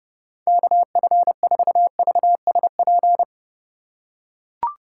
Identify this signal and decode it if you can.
Morse code, heard as a keyed tone, about 30 wpm, 700 Hz. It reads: KF4VSP